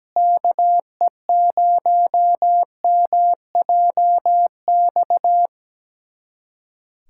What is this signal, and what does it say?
Morse code, heard as a keyed tone, about 17 wpm, 700 Hz. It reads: KE0MJX
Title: USA Callsign DE Callsign - 6 Characters - 17wpm